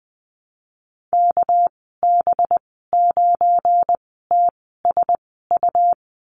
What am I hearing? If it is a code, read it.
Morse code, heard as a keyed tone, about 20 wpm, spaced as for 14 wpm, 700 Hz. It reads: KB9TSU
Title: 200 Lesson 32g - Letter k - Call Signs 20wpm